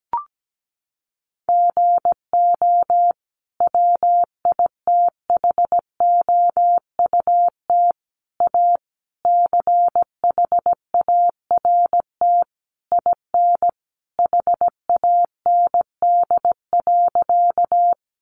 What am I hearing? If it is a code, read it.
Morse code, heard as a keyed tone, about 17 wpm, 700 Hz. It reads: GO WITHOUT A CHART IN HAND.
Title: Sentences from Top 1300 Words 17wpm